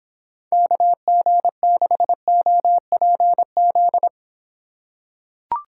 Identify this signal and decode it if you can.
Morse code, heard as a keyed tone, about 25 wpm, 700 Hz. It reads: KG6OPZ